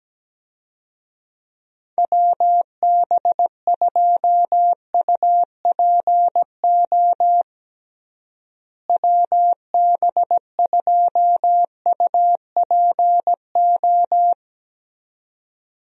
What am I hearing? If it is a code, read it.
Morse code, heard as a keyed tone, about 17 wpm, 700 Hz. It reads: WB2UPO WB2UPO
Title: USA Call Signs - 6 Characters - Repeated 3x - 17wpm